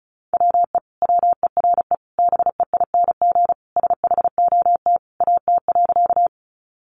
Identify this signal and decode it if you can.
Morse code, heard as a keyed tone, about 34 wpm, 700 Hz. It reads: WE WERE BEING SHOT AT.
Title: Sentences from Top 1100 Words 35wpm